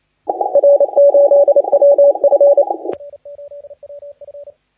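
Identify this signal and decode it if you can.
Morse code, heard as a keyed tone, about 28 wpm, 570 Hz, with radio noise. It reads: R8WF
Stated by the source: Morse contest audio recorded by LA4C during the SAC CW 2018 contest.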